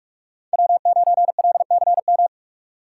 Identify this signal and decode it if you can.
Morse code, heard as a keyed tone, about 46 wpm, 700 Hz. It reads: W9LCM
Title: USA Call Signs - 2x Character Spacing - 45wpm